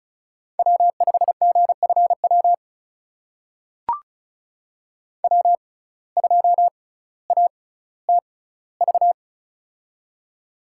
Morse code, heard as a keyed tone, about 36 wpm, spaced as for 18 wpm, 700 Hz. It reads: W5GFW W2ATV
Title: USA Call Signs - 5 Characters - 6x Character Spacing - 35wpm